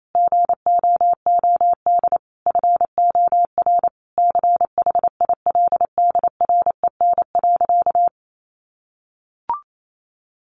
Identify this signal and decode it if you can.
Morse code, heard as a keyed tone, about 28 wpm, 700 Hz. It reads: GOOD FOR CHILDREN.
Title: Sentences from Top 400 Words 28wpm